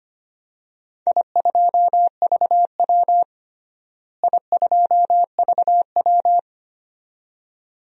Morse code, heard as a keyed tone, about 25 wpm, 700 Hz. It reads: I2VW I2VW
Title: DXCC-100 Call Signs - 4 Characters - Repeated 2x - 25wpm